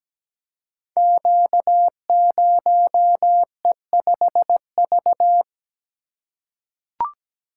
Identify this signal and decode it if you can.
Morse code, heard as a keyed tone, about 17 wpm, 700 Hz. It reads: Q0E5V
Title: Code Groups - 5 Letter:Numbers - 17wpm